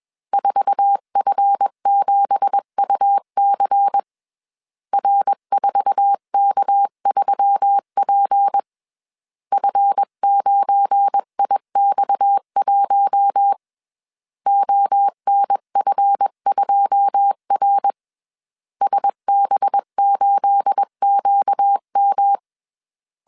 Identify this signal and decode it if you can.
Morse code, heard as a keyed tone, about 21 wpm, 800 Hz. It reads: 4F7UC R4K3P F9IX1 ONF2R SB8QM
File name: CW / DET / 1 9 1500C S B 20wpm 2313